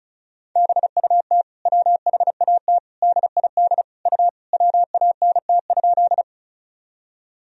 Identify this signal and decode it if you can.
Morse code, heard as a keyed tone, about 36 wpm, 700 Hz. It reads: BUT WHAT DID U WANT?
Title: Sentences from Top 200 Words and Abbreviations 35wpm